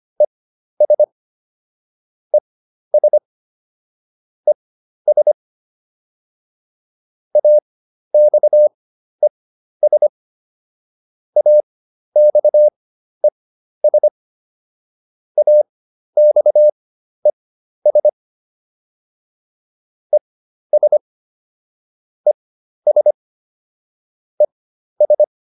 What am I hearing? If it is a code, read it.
Morse code, heard as a keyed tone, about 25 wpm, spaced as for 12 wpm, 600 Hz. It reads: ES ES ES AXES AXES AXES ES ES ES